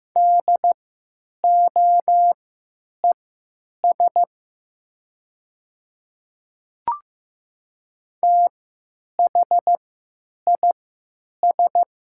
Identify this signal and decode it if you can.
Morse code, heard as a keyed tone, about 15 wpm, spaced as for 9 wpm, 700 Hz. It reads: DOES THIS
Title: 3x Character Spacing - Top 500 Words - 15wpm